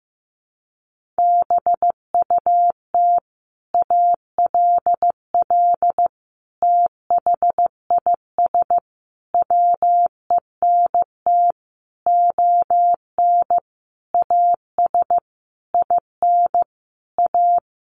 Morse code, heard as a keyed tone, about 15 wpm, 700 Hz. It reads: BUT ALL THIS WENT ON AS IN A
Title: Sentences from Top 1500 Words 15wpm